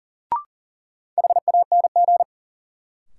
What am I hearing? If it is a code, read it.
Morse code, heard as a keyed tone, about 40 wpm, 700 Hz. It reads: HANG
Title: Sentences from Top 1600 Words - New Word Drill Alt - 40wpm